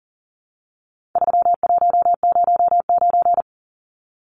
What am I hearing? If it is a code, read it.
Morse code, heard as a keyed tone, about 40 wpm, 700 Hz. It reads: 3109
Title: Numbers - 4-Digit 40wpm